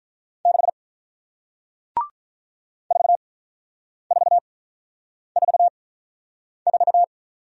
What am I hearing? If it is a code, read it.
Morse code, heard as a keyed tone, about 47 wpm, spaced as for 9 wpm, 690 Hz. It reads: 6 4444